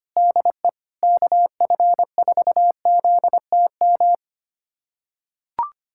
Morse code, heard as a keyed tone, about 25 wpm, 700 Hz. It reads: DE KF4ZTM
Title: USA Callsign DE Callsign - 6 Characters - 25wpm